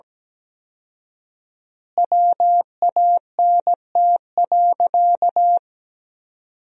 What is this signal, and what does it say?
Morse code, heard as a keyed tone, about 17 wpm, 700 Hz. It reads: WANT.
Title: 4x Word-Spacing - Sentences from Top 200 Words - 17wpm